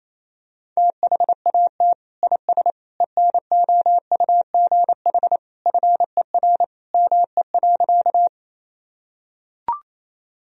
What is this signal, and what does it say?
Morse code, heard as a keyed tone, about 28 wpm, 700 Hz. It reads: THAT IS ENOUGH FER ME.
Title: Sentences from Top 400 Words and Abbreviations 28wpm